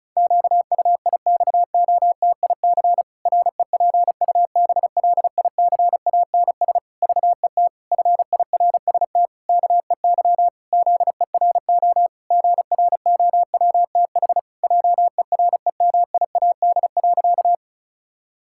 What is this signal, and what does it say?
Morse code, heard as a keyed tone, about 36 wpm, 700 Hz. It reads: QUIXOTIC REPUBLICANS VET FIRST KEY ZERO GROWTH JEREMIAD.
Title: Pangram Challenge v2 - 35wpm